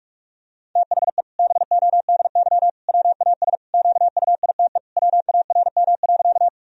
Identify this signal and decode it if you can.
Morse code, heard as a keyed tone, about 46 wpm, 700 Hz. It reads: THE BODY WAS QUITE WARM.
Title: Sentences from Top 1000 Words - Review Part 2 - 45wpm